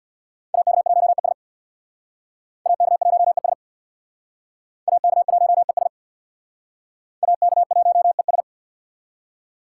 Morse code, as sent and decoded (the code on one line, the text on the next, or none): .- -.- .---- . ... / .- -.- .---- . ... / .- -.- .---- . ... / .- -.- .---- . ...
AK1ES AK1ES AK1ES AK1ES